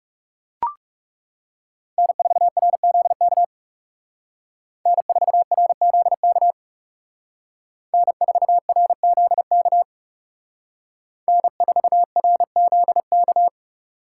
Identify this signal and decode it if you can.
Morse code, heard as a keyed tone, about 38 wpm, 700 Hz. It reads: N4RZK N4RZK N4RZK N4RZK